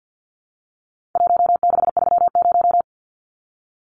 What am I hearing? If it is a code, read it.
Morse code, heard as a keyed tone, about 50 wpm, 700 Hz. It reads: 1630
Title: Numbers - Contest Style - 4-Digit 50wpm